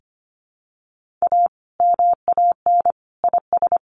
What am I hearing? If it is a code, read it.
Morse code, heard as a keyed tone, about 25 wpm, 700 Hz. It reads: A MAN IS
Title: Sentences from Top 1400 Words 25wpm